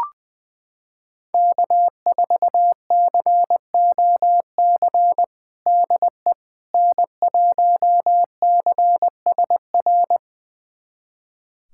Morse code, heard as a keyed tone, about 20 wpm, 700 Hz. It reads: K4COC DE N1CSR
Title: USA Callsign DE Callsign 20wpm